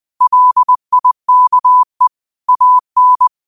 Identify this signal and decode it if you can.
Morse code, heard as a keyed tone, about 20 wpm, 1000 Hz. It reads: LIKE AN
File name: qotdfiles / 20wpm